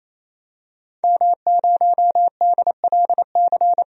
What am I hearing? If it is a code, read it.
Morse code, heard as a keyed tone, about 28 wpm, 700 Hz. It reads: M0DLC
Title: DXCC-100 Call Signs - 5 Characters - 28wpm